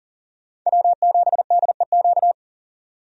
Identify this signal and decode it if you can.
Morse code, heard as a keyed tone, about 39 wpm, 700 Hz. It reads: W7DEQ
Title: USA Call Signs - 5 Characters - 40wpm